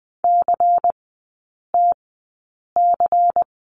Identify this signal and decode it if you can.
Morse code, heard as a keyed tone, about 20 wpm, spaced as for 8 wpm, 700 Hz. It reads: CTC
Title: 107 Lesson 18e - Letter m - Three Characters 20wpm